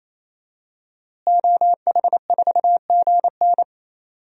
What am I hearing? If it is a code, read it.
Morse code, heard as a keyed tone, about 28 wpm, 700 Hz. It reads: OH4GN
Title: DXCC-100 Call Signs - 5 Characters - Repeated 3x - 28wpm